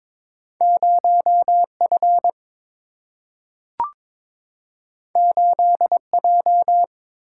Morse code, heard as a keyed tone, about 22 wpm, 700 Hz. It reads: 0F 8J
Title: Single-Letter Single-Number 22wpm